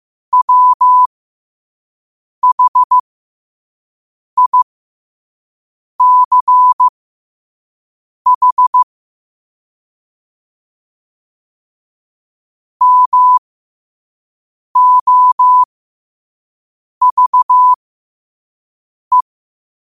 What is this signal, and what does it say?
Morse code, heard as a keyed tone, about 15 wpm, spaced as for 5 wpm, 1000 Hz. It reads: WHICH MOVE
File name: qotdfiles / 5wpm